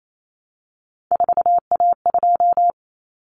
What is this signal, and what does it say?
Morse code, heard as a keyed tone, about 28 wpm, 700 Hz. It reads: 4A2
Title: Common Cut Numbers - 3-Digit 28wpm